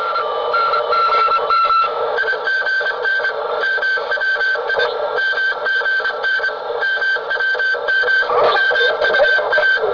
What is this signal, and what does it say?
Morse code, heard as a keyed tone, about 25 wpm, 1550 Hz, with radio noise. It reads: IGN MWI MGN MWM MUA
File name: luxorion / Radio / cosmos376-digital-telemetry-1970